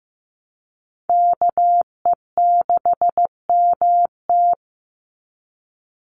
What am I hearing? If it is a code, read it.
Morse code, heard as a keyed tone, about 15 wpm, 700 Hz. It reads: KE6MT
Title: NA SOTA Chaser Callsigns 15wpm